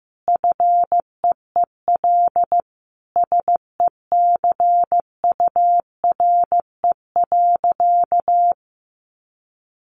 Morse code, heard as a keyed tone, about 15 wpm, 700 Hz. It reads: FEEL SECURE.
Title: Sentences from Top 1500 Words 15wpm